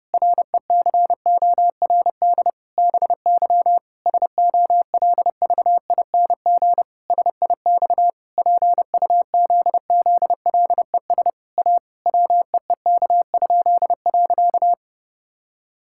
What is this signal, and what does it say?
Morse code, heard as a keyed tone, about 28 wpm, 700 Hz. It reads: RECORD BY SOLVING SIX PUZZLES A WEEK?.
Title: Pangram Challenge v2 - 30wpm